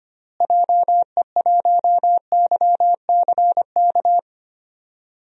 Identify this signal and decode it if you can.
Morse code, heard as a keyed tone, about 25 wpm, 700 Hz. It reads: JE1YCK
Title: DXCC-100 Call Signs - 6 Characters - Repeated 3x - 25wpm